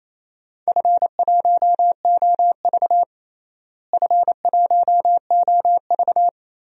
Morse code, heard as a keyed tone, about 28 wpm, 700 Hz. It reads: F1OV F1OV
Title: DXCC-100 Call Signs - 4 Characters - Repeated 3x - 28wpm